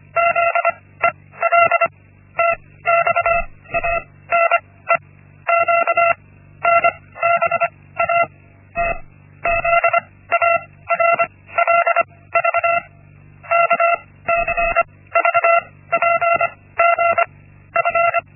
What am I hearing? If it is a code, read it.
Morse code, heard as a keyed tone, about 25 wpm, 680 Hz. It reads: Z E L T X A N E Q N B A T Z A R L V K C V P G F
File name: Linksmakalnis-1